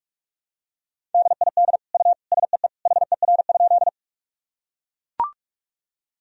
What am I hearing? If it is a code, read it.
Morse code, heard as a keyed tone, about 42 wpm, 700 Hz. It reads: DID U SEE HER?
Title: Sentences from Top 100 Words and Abbreviations 45wpm